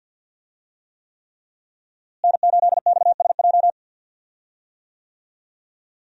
Morse code, heard as a keyed tone, about 50 wpm, 700 Hz. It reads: N8XSJ
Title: USA Call Signs - 5 Characters - Repeated 3x - 50wpm